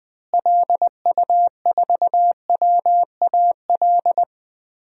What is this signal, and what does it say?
Morse code, heard as a keyed tone, about 19 wpm, 700 Hz. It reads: LU4WAL
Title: DXCC-100 Call Signs - 6 Characters - Repeated 3x - 20wpm